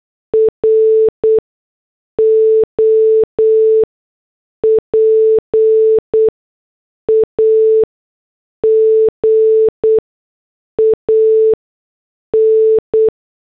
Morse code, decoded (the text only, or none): ROPAGAN